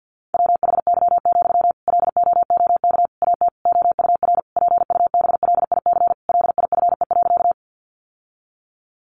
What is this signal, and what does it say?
Morse code, heard as a keyed tone, about 48 wpm, 700 Hz. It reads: WHY, LOOK AT OUR PUBLIC LIFE.
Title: Sentences from Top 600 Words 50wpm